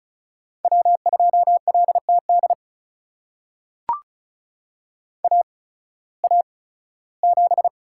Morse code, heard as a keyed tone, about 36 wpm, spaced as for 19 wpm, 700 Hz. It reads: W2LTD AA7
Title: USA Call Signs - 5 Characters - 8x Character Spacing - 35wpm